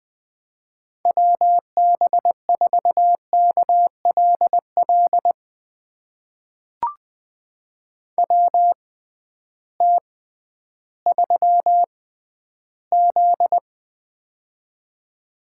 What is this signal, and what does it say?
Morse code, heard as a keyed tone, about 20 wpm, spaced as for 12 wpm, 700 Hz. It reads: WB4KLL WT3Z
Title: USA Call Signs - 6x Character Spacing - 20wpm